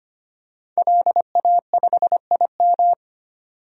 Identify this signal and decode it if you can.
Morse code, heard as a keyed tone, about 25 wpm, 700 Hz. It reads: LA5IM
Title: DXCC-100 Call Signs - 5 Characters - 25wpm